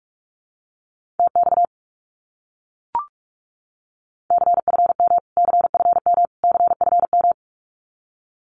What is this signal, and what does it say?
Morse code, heard as a keyed tone, about 44 wpm, 700 Hz. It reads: TX CFM CFM CFM
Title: Abbreviations Repeated 3x - 45wpm